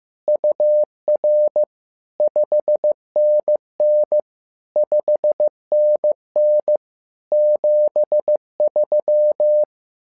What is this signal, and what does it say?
Morse code, heard as a keyed tone, about 15 wpm, 600 Hz. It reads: UR 5NN 5NN 73
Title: SOTA QSO Practice 15wpm